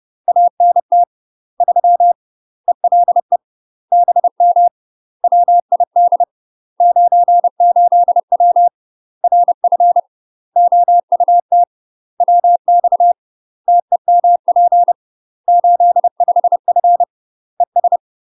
Morse code, heard as a keyed tone, about 30 wpm, 700 Hz. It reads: ANT 3 ELE BM WID 98W RF OUT WX TEMP 85F ES
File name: UK MASTER QSOs TRAINING / 30WPM / UK MASTER QSOs 0019